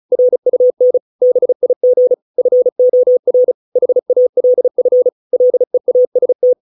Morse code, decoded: RUN BIG FOR HALF LEAST

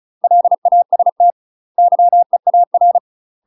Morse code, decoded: LAST YEAR

